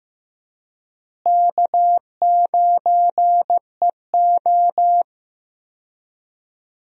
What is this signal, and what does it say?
Morse code, heard as a keyed tone, about 15 wpm, 700 Hz. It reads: K9EO
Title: USA Call Signs - 4 Characters - Repeated 3x - 15wpm